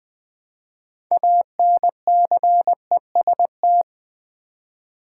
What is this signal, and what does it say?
Morse code, heard as a keyed tone, about 20 wpm, 700 Hz. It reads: ANCEST
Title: Build A Word - 3 Letters At A Time - 8-Letter Words - 20wpm